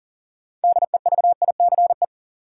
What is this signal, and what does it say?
Morse code, heard as a keyed tone, about 40 wpm, 700 Hz. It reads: DEVICE
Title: Sentences from Top 1500 Words - New Word Drill Alt - 40wpm